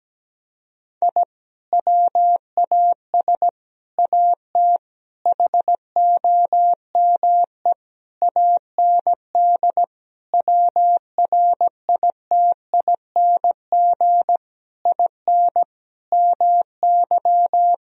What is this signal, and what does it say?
Morse code, heard as a keyed tone, about 17 wpm, 700 Hz. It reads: I WAS AT HOME AND WRITING IN MY
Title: Sentences from Top 1000 Words - Review Part 1 - 17wpm